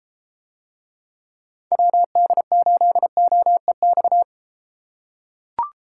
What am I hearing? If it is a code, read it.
Morse code, heard as a keyed tone, about 32 wpm, 700 Hz. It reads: WD8OEX